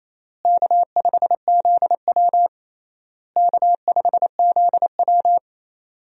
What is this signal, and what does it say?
Morse code, heard as a keyed tone, about 28 wpm, 700 Hz. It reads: K5ZW K5ZW
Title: USA Call Signs - 4 Characters - Repeated 3x - 28wpm